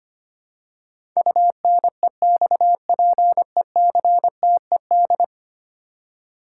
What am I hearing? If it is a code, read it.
Morse code, heard as a keyed tone, about 25 wpm, 700 Hz. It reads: UNEXPECTED
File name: 10 Letter Words - First 3 Letters Given - 25wpm